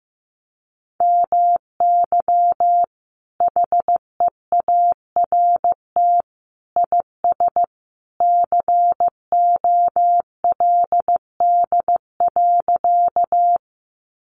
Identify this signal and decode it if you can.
Morse code, heard as a keyed tone, about 15 wpm, 700 Hz. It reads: MY HEART IS COLD.
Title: Sentences from Top 1200 Words 15wpm